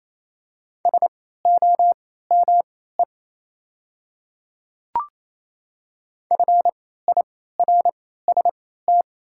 Morse code, heard as a keyed tone, about 28 wpm, spaced as for 16 wpm, 700 Hz. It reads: SOME FIRST
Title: 3x Character Spacing - Top 100 Words - 28wpm